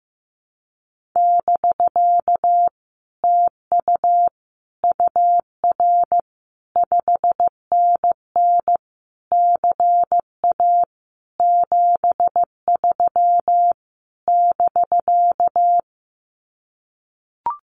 Morse code, heard as a keyed tone, about 15 wpm, 700 Hz. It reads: <BK> TU UR 5NN CA 73 <BK>
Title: Common QSO Phrases Encore 15wpm